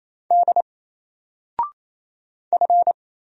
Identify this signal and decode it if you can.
Morse code, heard as a keyed tone, about 27 wpm, 700 Hz. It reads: D F
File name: Single-Letter-Number-28wpm